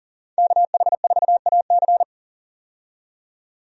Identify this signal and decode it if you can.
Morse code, heard as a keyed tone, about 40 wpm, 700 Hz. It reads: KH4AC
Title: USA Call Signs - 5 Characters - 40wpm